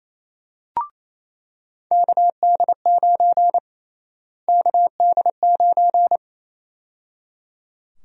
Morse code, heard as a keyed tone, about 27 wpm, 700 Hz. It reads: KD9 KD9
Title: USA Call Sign Prefixes - Repeated 2x - 28wpm